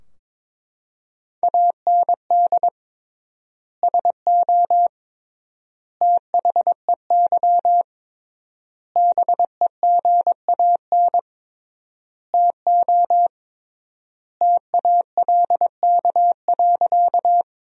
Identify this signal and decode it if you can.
Morse code, heard as a keyed tone, about 22 wpm, 700 Hz. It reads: AND SO THEY BEGAN TO TALK.
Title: 3x Word-Spacing - Sentences from Top 400 Words - 22wpm